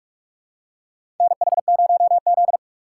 Morse code, heard as a keyed tone, about 46 wpm, 700 Hz. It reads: NH0Z
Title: USA Call Signs - 4 Characters - 6x Character Spacing - 45wpm